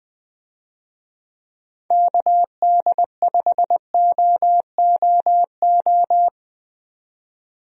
Morse code, heard as a keyed tone, about 20 wpm, 700 Hz. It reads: KD5OOO
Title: USA Call Signs - 6 Characters - 20wpm